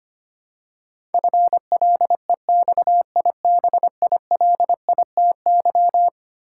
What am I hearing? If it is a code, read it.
Morse code, heard as a keyed tone, about 25 wpm, 700 Hz. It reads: FLEXIBILITY
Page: morse-code-ninja-practice-25wpm